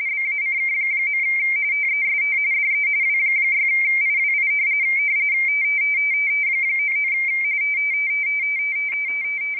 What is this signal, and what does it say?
RTTY, about 45.45 baud. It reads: BACK. TEST DE W6IWI 99
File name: rtty / websdr recording 14086.0kHz